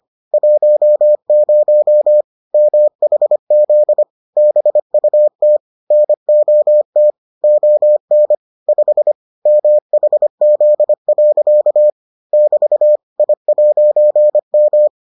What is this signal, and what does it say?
Morse code, heard as a keyed tone, about 25 wpm, 600 Hz. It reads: 10 MHZ BUT NOT ON 5 MHZ. <BT> I'M